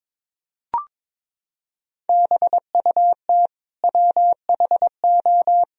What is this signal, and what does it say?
Morse code, heard as a keyed tone, about 21 wpm, 700 Hz. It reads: BUT WHO
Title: Sentences from Top 1000 Words - Review Part 2 - 22wpm